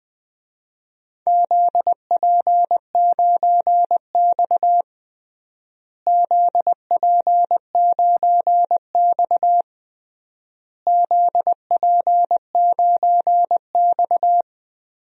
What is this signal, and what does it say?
Morse code, heard as a keyed tone, about 20 wpm, 700 Hz. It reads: ZP9X ZP9X ZP9X
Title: DXCC-100 Call Signs - 4 Characters - Repeated 3x - 20wpm